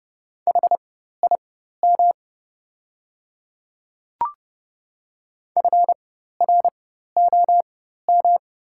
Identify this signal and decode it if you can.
Morse code, heard as a keyed tone, about 30 wpm, spaced as for 14 wpm, 700 Hz. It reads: HIM FROM
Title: 4x Character Spacing - Top 300 Words - 30wpm